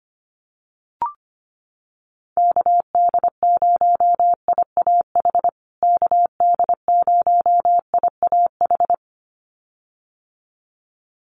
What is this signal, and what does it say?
Morse code, heard as a keyed tone, about 25 wpm, 700 Hz. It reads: KD0IAH KD0IAH
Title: USA Call Signs Repeated 2x 25wpm